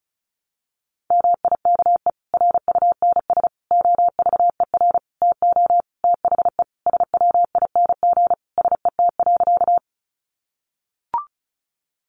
Morse code, heard as a keyed tone, about 34 wpm, 700 Hz. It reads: MIKE RUNS OVER TO THE SWING SET.
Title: Additional Easy Phrases 35wpm